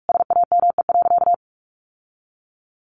Morse code, heard as a keyed tone, about 44 wpm, 700 Hz. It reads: SAME.
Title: Sentences from Top 1200 Words 45wpm